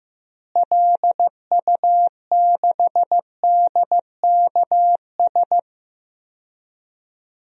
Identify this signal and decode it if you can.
Morse code, heard as a keyed tone, about 15 wpm, 700 Hz. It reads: LU6DKS
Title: DXCC-100 Call Signs - 6 Characters - Repeated 3x - 15wpm